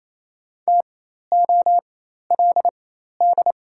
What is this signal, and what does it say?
Morse code, heard as a keyed tone, about 27 wpm, spaced as for 13 wpm, 700 Hz. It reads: TOLD
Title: 4x Character Spacing - Top 500 Words - 28wpm